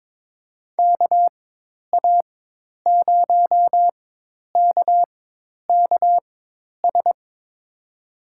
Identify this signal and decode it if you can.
Morse code, heard as a keyed tone, about 23 wpm, spaced as for 10 wpm, 700 Hz. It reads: KA0KKS
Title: USA Call Signs - 4x Character Spacing - 22wpm